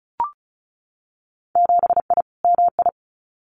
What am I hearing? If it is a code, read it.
Morse code, heard as a keyed tone, about 36 wpm, 700 Hz. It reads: 7I MI
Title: Winter Field Day Exchanges Encore 35wpm